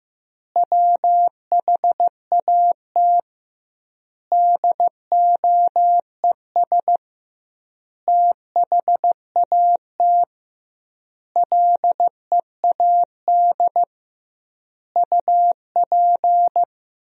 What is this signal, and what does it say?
Morse code, heard as a keyed tone, about 15 wpm, 700 Hz. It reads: WHAT DOES THAT LEAD UP
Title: 2x Word-Spacing - Sentences from Top 500 Words - 15wpm